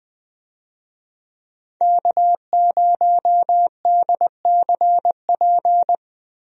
Morse code, heard as a keyed tone, about 20 wpm, 700 Hz. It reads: K0DCP
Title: USA Call Signs - 5 Characters - Repeated 2x - 20wpm